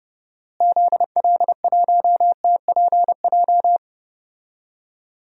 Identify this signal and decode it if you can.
Morse code, heard as a keyed tone, about 29 wpm, 700 Hz. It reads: ZL1TPJ
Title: DXCC-100 Call Signs - 6 Characters - Repeated 3x - 30wpm